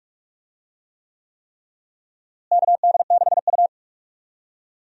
Morse code, heard as a keyed tone, about 46 wpm, 700 Hz. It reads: KD6U
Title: USA Call Signs - 4 Characters - Repeated 3x - 45wpm